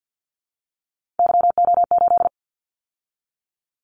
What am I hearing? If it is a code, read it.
Morse code, heard as a keyed tone, about 50 wpm, 700 Hz. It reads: YO8
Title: Callsign Prefixes for Top 25 DXCC - 50wpm